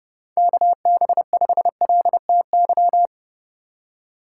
Morse code, heard as a keyed tone, about 30 wpm, 700 Hz. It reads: KB5LTY
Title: USA Call Signs - 6 Characters - 4x Character Spacing - 30wpm